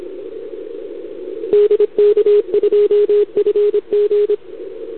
Morse code, heard as a keyed tone, about 26 wpm, 410 Hz, with radio noise. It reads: DK2FG
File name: CQWWCW2021 / JA3YBK